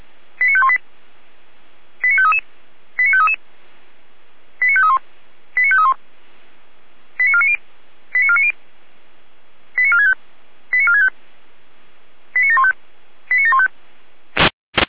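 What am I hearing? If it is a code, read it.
Five-tone selective call (ZVEI), heard as quick runs of five tones, tones of about 70 ms. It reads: 89518 89533 89533 89531 89531 89490 89490 89465 89465 89715 89715
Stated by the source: SDR-received ZVEI five-tone selective call, code 89485.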